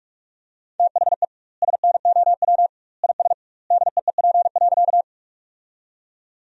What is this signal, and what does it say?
Morse code, heard as a keyed tone, about 46 wpm, 700 Hz. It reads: THE SNOW IS DEEP.
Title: 2x Word-Spacing - Sentences from Top 500 Words - 45wpm